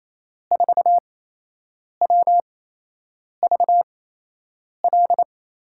Morse code, heard as a keyed tone, about 28 wpm, spaced as for 8 wpm, 700 Hz. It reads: 4WVL
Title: USA Call Signs - 6 Characters - 8x Character Spacing - 28wpm